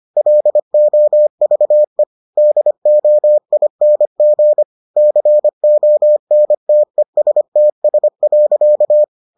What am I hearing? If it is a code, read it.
Morse code, heard as a keyed tone, about 25 wpm, 600 Hz. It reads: LOVE DOING CONTESTS.